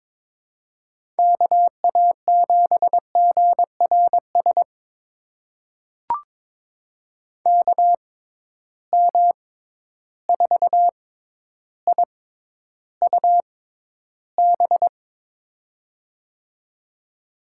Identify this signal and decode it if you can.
Morse code, heard as a keyed tone, about 22 wpm, spaced as for 11 wpm, 700 Hz. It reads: KA7GRS KM4IUB
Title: USA Call Signs - 6 Characters - 6x Character Spacing - 22wpm